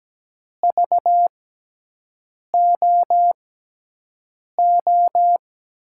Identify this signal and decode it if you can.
Morse code, heard as a keyed tone, about 17 wpm, spaced as for 6 wpm, 700 Hz. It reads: VOO
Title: USA Call Signs - 6x Character Spacing - 17wpm